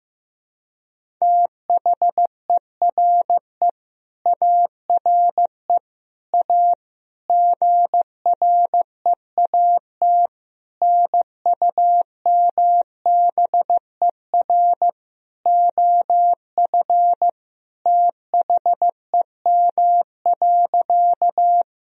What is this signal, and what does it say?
Morse code, heard as a keyed tone, about 15 wpm, 700 Hz. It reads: THERE ARE A GREAT NUMBER OF THEM.
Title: Sentences from Top 1000 Words - Review Part 1 - 15wpm